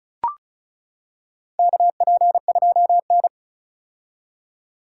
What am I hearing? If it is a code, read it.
Morse code, heard as a keyed tone, about 36 wpm, 700 Hz. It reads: KP2N
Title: K1USN SST QSO Practice - 35wpm